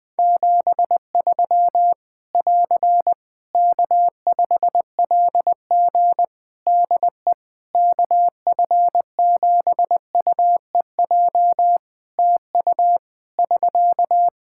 Morse code, heard as a keyed tone, about 20 wpm, 700 Hz. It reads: 73 <AR> K5LG DE KF7UEJ TU <SK>
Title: K1USN SST QSO Practice 20wpm